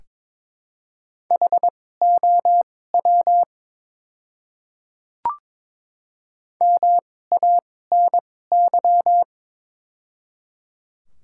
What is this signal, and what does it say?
Morse code, heard as a keyed tone, about 22 wpm, spaced as for 16 wpm, 700 Hz. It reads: HOW MANY
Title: 2x Character Spacing - Top 200 Words - 22wpm